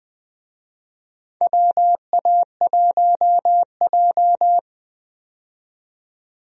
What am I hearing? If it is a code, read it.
Morse code, heard as a keyed tone, about 21 wpm, 700 Hz. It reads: WA1J
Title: USA Call Signs - 4 Characters - 20wpm